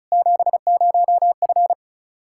Morse code, heard as a keyed tone, about 36 wpm, 700 Hz. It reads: 70F
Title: Practice QSO - Exchange 2 - Info - 35wpm